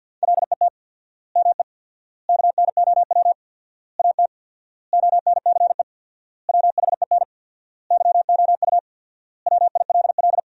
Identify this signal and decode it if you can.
Morse code, heard as a keyed tone, about 50 wpm, 700 Hz. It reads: LET ME KNOW AT ONCE WHEN YOU WILL
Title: 4x Word-Spacing - Sentences from Top 300 Words - 50wpm